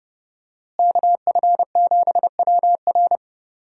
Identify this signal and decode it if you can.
Morse code, heard as a keyed tone, about 30 wpm, 700 Hz. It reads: KF7WR
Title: USA Call Signs - 5 Characters - Repeated 3x - 30wpm